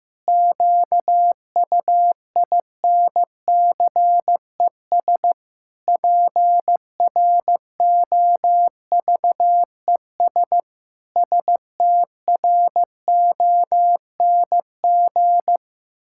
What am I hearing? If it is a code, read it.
Morse code, heard as a keyed tone, about 15 wpm, 700 Hz. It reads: QUINCES PROVES STRONG
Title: Pangram Challenge v2 - 15wpm